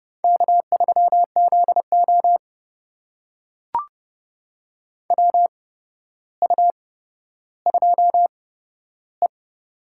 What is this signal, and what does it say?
Morse code, heard as a keyed tone, about 31 wpm, spaced as for 13 wpm, 700 Hz. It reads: K3ZO WU2E
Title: USA Call Signs - 4 Characters - 8x Character Spacing - 30wpm